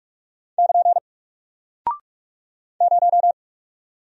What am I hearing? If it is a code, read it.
Morse code, heard as a keyed tone, about 45 wpm, 700 Hz. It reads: <KN> 0
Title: Single Character (Letter:Number:Punctuation:Prosign) Alt 45wpm